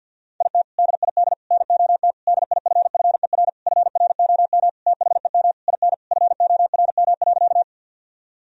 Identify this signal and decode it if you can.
Morse code, heard as a keyed tone, about 51 wpm, 700 Hz. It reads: IT DID NOT DIFFER FROM THEM IN FORM.
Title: Sentences from Top 1000 Words - Review Part 1 - 50wpm